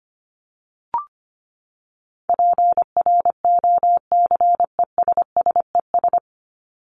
Morse code, heard as a keyed tone, about 25 wpm, 700 Hz. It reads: PROCESSES
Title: ES:S Suffix - Top 100 Words with Base Words 25wpm